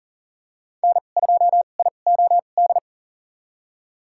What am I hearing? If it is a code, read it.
Morse code, heard as a keyed tone, about 40 wpm, spaced as for 29 wpm, 700 Hz. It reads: N2IOD
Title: USA Call Signs - 2x Character Spacing - 40wpm